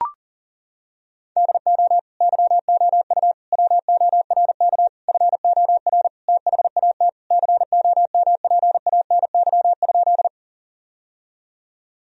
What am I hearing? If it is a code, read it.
Morse code, heard as a keyed tone, about 40 wpm, 700 Hz. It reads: DO YOU WORK FOR THAT COMPANY?
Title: Sentences from Top 1000 Words - Review Part 3 - 40wpm